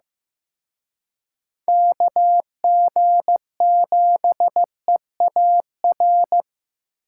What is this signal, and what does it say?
Morse code, heard as a keyed tone, about 18 wpm, 700 Hz. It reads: KG7EAR
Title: Speed Racing - USA 2x3 Call Signs 15wpm